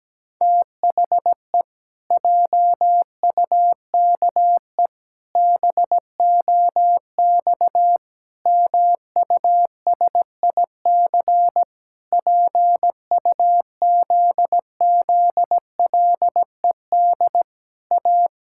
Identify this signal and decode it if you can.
Morse code, heard as a keyed tone, about 17 wpm, 700 Hz. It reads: THE JUKE BOX MUSIC PUZZLED A
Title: Pangram Challenge v2 - 17wpm